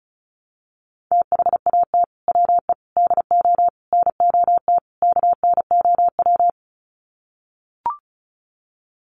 Morse code, heard as a keyed tone, about 34 wpm, 700 Hz. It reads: THAT WE DO NOT KNOW
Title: Sets of 5 Words Encore2 35wpm